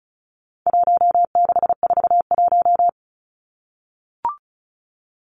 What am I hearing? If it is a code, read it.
Morse code, heard as a keyed tone, about 34 wpm, 700 Hz. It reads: 1641
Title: Numbers - Contest Style - 4-Digit 35wpm